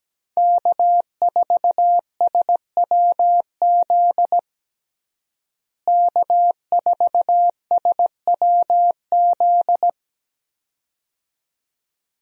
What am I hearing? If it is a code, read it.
Morse code, heard as a keyed tone, about 17 wpm, 700 Hz. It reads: K4SWZ K4SWZ
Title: USA Call Signs - 5 Characters - Repeated 2x - 17wpm